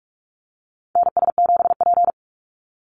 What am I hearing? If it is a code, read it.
Morse code, heard as a keyed tone, about 44 wpm, 700 Hz. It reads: NS7P
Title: NA SOTA Activator Callsigns 45wpm